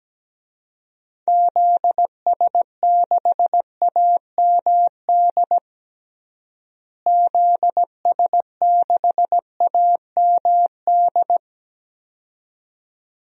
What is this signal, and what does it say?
Morse code, heard as a keyed tone, about 17 wpm, 700 Hz. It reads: ZS6AMD ZS6AMD
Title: DXCC-100 Call Signs - 6 Characters - Repeated 2x - 17wpm